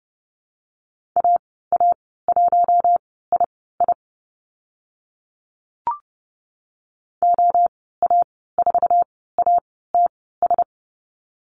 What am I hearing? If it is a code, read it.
Morse code, heard as a keyed tone, about 30 wpm, spaced as for 17 wpm, 690 Hz. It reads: AA1II OA4ATS
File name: Lesson 15g - Number 5 - Call Signs 30wpm